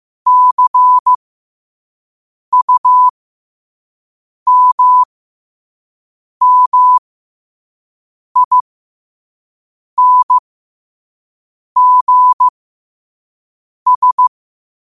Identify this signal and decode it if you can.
Morse code, heard as a keyed tone, about 15 wpm, spaced as for 5 wpm, 1000 Hz. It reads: CUMMINGS
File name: qotdfiles / 5wpm